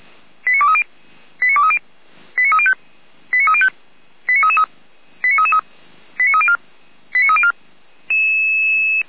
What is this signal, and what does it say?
Five-tone selective call (ZVEI), heard as quick runs of five tones, tones of about 70 ms. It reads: 89230 89230 89305 89305 89333 89333 89304 89304